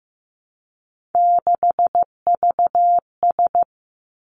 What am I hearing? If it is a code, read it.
Morse code, heard as a keyed tone, about 15 wpm, 700 Hz. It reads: 6VS
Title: Three Challenging Letter-Number Combinations 15wpm